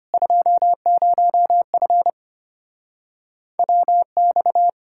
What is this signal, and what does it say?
Morse code, heard as a keyed tone, about 33 wpm, 700 Hz. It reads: 20F WX